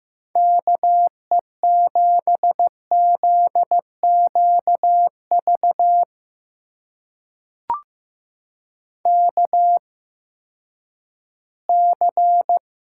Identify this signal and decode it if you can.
Morse code, heard as a keyed tone, about 15 wpm, spaced as for 10 wpm, 700 Hz. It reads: KE7ZQV KC
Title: USA Call Signs - 6 Characters - 8x Character Spacing - 15wpm